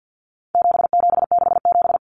Morse code, heard as a keyed tone, about 50 wpm, 700 Hz. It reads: 7767
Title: Numbers - 4-Digit 50wpm